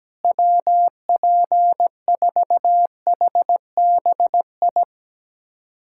Morse code, heard as a keyed tone, about 17 wpm, 700 Hz. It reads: WP4HBI
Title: USA Callsign DE Callsign 17wpm